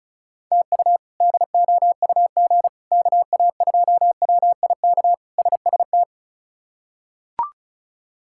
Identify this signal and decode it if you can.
Morse code, heard as a keyed tone, about 36 wpm, 700 Hz. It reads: TU DOUG KA2WIK SST
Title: K1USN SST QSO Practice - 35wpm